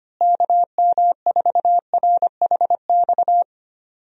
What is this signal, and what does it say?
Morse code, heard as a keyed tone, about 25 wpm, 700 Hz. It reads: KM4RHX
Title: USA Call Signs - 6 Characters - Repeated 3x - 25wpm